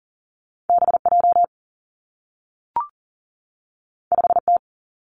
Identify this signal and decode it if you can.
Morse code, heard as a keyed tone, about 42 wpm, 700 Hz. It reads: B J 5 T
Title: Two Challenging Letter-Number Combinations 40wpm